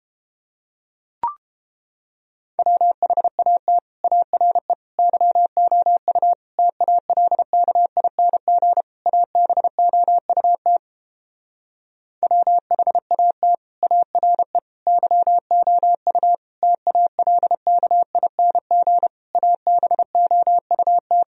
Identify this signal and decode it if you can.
Morse code, heard as a keyed tone, about 30 wpm, 700 Hz. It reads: WHAT ARE YOU TALKING ABOUT WHAT ARE YOU TALKING ABOUT